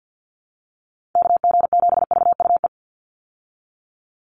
Morse code, heard as a keyed tone, about 50 wpm, 700 Hz. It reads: KG7VUE
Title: USA Call Signs 50wpm